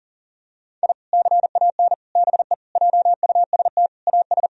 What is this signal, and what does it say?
Morse code, heard as a keyed tone, about 40 wpm, 700 Hz. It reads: I CAN BE JUST AS
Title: Sentences from Top 1000 Words - Review Part 3 - 40wpm